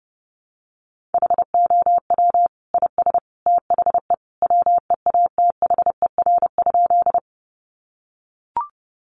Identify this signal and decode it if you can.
Morse code, heard as a keyed tone, about 30 wpm, 700 Hz. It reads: HOW IS THE WEATHER?
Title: Sentences from Top 800 Words 30wpm